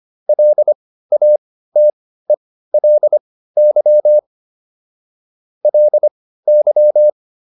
Morse code, heard as a keyed tone, about 25 wpm, spaced as for 15 wpm, 600 Hz. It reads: LATELY LY